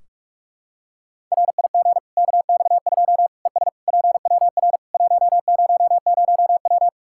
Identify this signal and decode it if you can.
Morse code, heard as a keyed tone, about 46 wpm, 700 Hz. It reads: RIG KX2 ES PWR 100W
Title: K1USN SST QSO Practice 45wpm